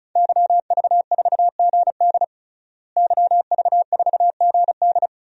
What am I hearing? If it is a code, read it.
Morse code, heard as a keyed tone, about 34 wpm, 700 Hz. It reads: YV4GD YV4GD
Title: DXCC-100 Call Signs - 5 Characters - Repeated 2x - 35wpm